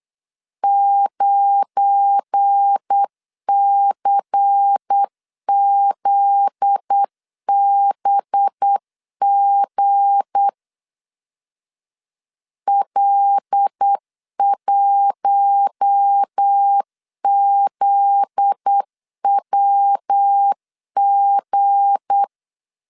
Morse code, heard as a keyed tone, about 8 wpm, 800 Hz. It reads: 9CZBG L1ZWG
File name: LEZIONI / 7 GF9Z / 7 GF9Z 8wpm 3213